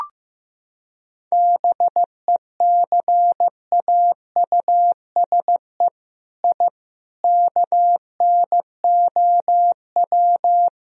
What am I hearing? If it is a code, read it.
Morse code, heard as a keyed tone, about 15 wpm, 700 Hz. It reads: BECAUSE I KNOW
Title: Sentences from Top 1000 Words - Review Part 3 - 15wpm